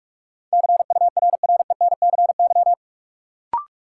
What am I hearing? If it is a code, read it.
Morse code, heard as a keyed tone, about 47 wpm, 700 Hz. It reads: CURRENCY
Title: Sentences from Top 1500 Words - New Word Drill Alt - 45wpm